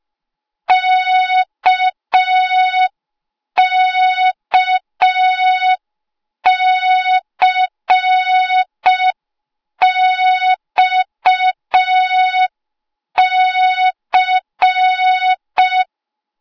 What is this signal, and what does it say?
Morse code, heard as a keyed tone, about 5 wpm, 3700 Hz. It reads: KKCXC
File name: Morze-HA2MN / hang / 08KXC HA2MN